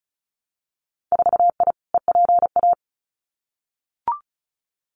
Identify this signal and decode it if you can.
Morse code, heard as a keyed tone, about 36 wpm, 700 Hz. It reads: 4I EPA
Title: Winter Field Day Exchanges Encore 35wpm